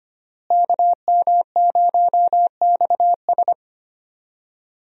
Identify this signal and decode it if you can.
Morse code, heard as a keyed tone, about 25 wpm, 700 Hz. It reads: KM0XS
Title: USA Call Signs - 5 Characters - Repeated 2x - 25wpm